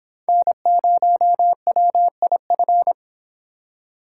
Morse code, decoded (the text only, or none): N0WIF